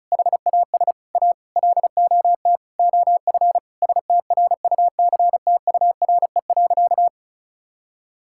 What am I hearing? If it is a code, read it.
Morse code, heard as a keyed tone, about 36 wpm, 700 Hz. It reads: HAS A LOT OF STRUCTURE.
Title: Sentences from Top 1000 Words - Review Part 6 - 35wpm